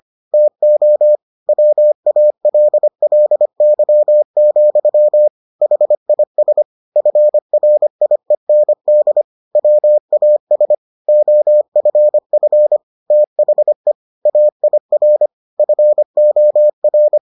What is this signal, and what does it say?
Morse code, heard as a keyed tone, about 25 wpm, 600 Hz. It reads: TO WALLY, HIS FRIEND WAS OFF THE AIR FOR